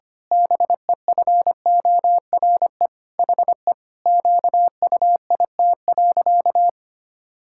Morse code, heard as a keyed tone, about 25 wpm, 700 Hz. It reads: BEFORE HE QUIT.
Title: Pangram Challenge v2 - 25wpm